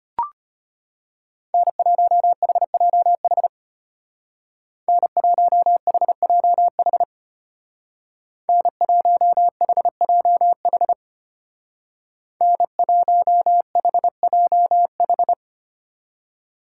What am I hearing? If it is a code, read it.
Morse code, heard as a keyed tone, about 31 wpm, 700 Hz. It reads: N1HJH N1HJH N1HJH N1HJH
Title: Speed Racing - USA 1x3 Call Signs 25wpm